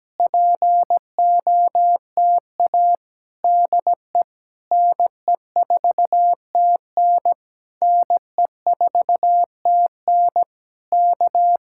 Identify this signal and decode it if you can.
Morse code, heard as a keyed tone, about 17 wpm, 700 Hz. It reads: POTA DE NE4TN NE4TN K
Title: Speed-Racing Common QSO Phrases - 17wpm